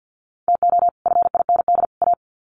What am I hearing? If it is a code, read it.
Morse code, heard as a keyed tone, about 50 wpm, 700 Hz. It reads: TO FIND A
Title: Sets of 3 Words 50wpm